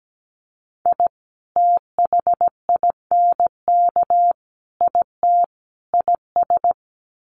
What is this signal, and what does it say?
Morse code, heard as a keyed tone, about 17 wpm, 700 Hz. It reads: I THINK IT IS
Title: Sets of 4 Words 17wpm